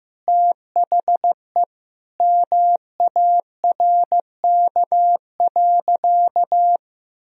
Morse code, heard as a keyed tone, about 15 wpm, 700 Hz. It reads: THE MARK.
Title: Sentences from Top 1000 Words - Review Part 1 - 15wpm